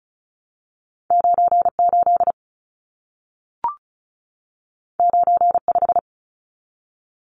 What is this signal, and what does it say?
Morse code, heard as a keyed tone, about 36 wpm, 700 Hz. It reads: 98 95
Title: Numbers - 2-Digit 35wpm